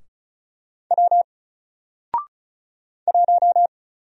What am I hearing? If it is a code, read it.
Morse code, heard as a keyed tone, about 34 wpm, 700 Hz. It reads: W 1
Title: Single Letter-Number Alt 35wpm